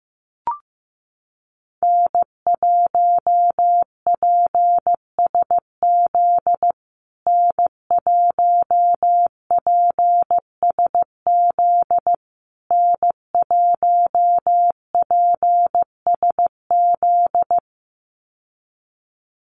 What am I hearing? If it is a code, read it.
Morse code, heard as a keyed tone, about 15 wpm, 700 Hz. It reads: N1PSZ N1PSZ N1PSZ
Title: USA Call Signs Repeated 3x 15wpm